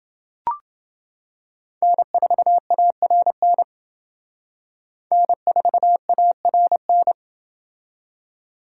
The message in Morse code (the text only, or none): N4ARN N4ARN